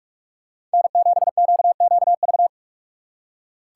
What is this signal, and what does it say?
Morse code, heard as a keyed tone, about 46 wpm, 700 Hz. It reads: N7QQV
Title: USA Call Signs - 5 Characters - 4x Character Spacing - 45wpm